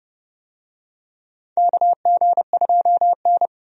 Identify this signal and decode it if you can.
Morse code, heard as a keyed tone, about 30 wpm, 700 Hz. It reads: KG2N
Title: USA Call Signs - 4 Characters - 30wpm